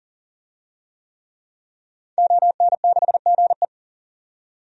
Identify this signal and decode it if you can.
Morse code, heard as a keyed tone, about 38 wpm, 700 Hz. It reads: ON6GE
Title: DXCC-100 Call Signs - 5 Characters - 40wpm